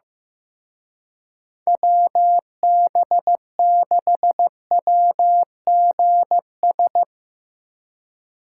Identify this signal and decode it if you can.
Morse code, heard as a keyed tone, about 16 wpm, 700 Hz. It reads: WB6WGS
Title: Speed Racing - USA 2x3 Call Signs 15wpm